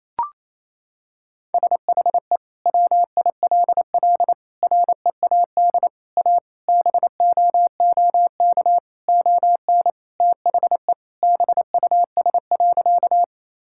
Morse code, heard as a keyed tone, about 27 wpm, 700 Hz. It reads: SHE WILL READ A BOOK ON THE BUS.
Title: Sentences from Top 1000 Words - Review Part 6 - 28wpm